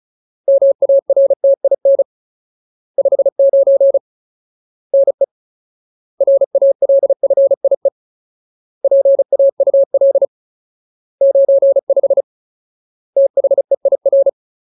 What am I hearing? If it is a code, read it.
Morse code, heard as a keyed tone, about 34 wpm, 550 Hz. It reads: MARTIN 59 NE RALFIE PAUL 95 THEIR
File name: Ultimate Word List / 35WPM Ultimate